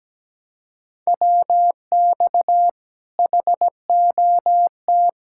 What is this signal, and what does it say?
Morse code, heard as a keyed tone, about 17 wpm, 700 Hz. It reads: WX HOT
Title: Speed-Racing Common QSO Phrases - 17wpm